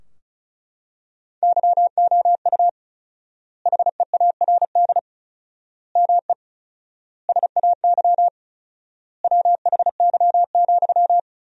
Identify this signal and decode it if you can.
Morse code, heard as a keyed tone, about 36 wpm, 700 Hz. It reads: YOU HEARD ME SAY WHY,
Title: 4x Word-Spacing - Sentences from Top 500 Words - 35wpm